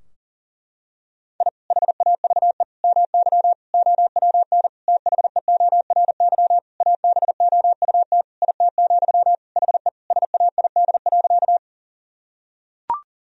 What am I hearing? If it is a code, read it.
Morse code, heard as a keyed tone, about 40 wpm, 700 Hz. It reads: I HAVE MY OWN THEORY ABOUT IT, HE SAID.
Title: Sentences from Top 1000 Words - Review Part 3 - 40wpm